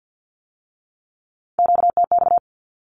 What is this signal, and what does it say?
Morse code, heard as a keyed tone, about 50 wpm, 700 Hz. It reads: QTX
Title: Q-Codes with Spoken Meaning - 50wpm